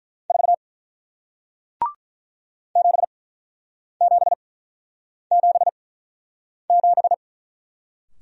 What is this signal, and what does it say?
Morse code, heard as a keyed tone, about 48 wpm, spaced as for 9 wpm, 700 Hz. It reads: 4 7777